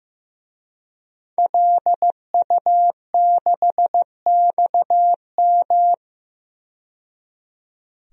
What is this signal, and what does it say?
Morse code, heard as a keyed tone, about 15 wpm, 700 Hz. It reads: LU6XM
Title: DXCC-100 Call Signs - 5 Characters - 15wpm